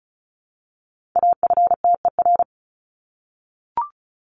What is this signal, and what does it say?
Morse code, heard as a keyed tone, about 34 wpm, 700 Hz. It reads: AFTER
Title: Top 200 Words 35wpm